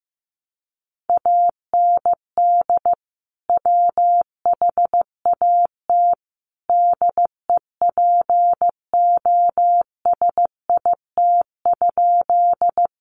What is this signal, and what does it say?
Morse code, heard as a keyed tone, about 15 wpm, 700 Hz. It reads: AND WHAT DEPOSIT?
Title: Sentences from Top 1300 Words 15wpm